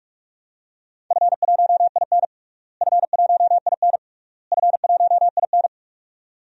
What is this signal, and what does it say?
Morse code, heard as a keyed tone, about 43 wpm, 700 Hz. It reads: F1IN F1IN F1IN
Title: DXCC-100 Call Signs - 4 Characters - Repeated 3x - 45wpm